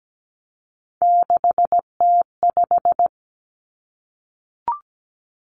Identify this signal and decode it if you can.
Morse code, heard as a keyed tone, about 17 wpm, 700 Hz. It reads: 6T5
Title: Three Challenging Letter-Number Combinations 17wpm